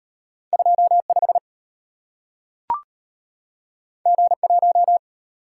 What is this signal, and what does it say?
Morse code, heard as a keyed tone, about 35 wpm, 700 Hz. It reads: 25 G1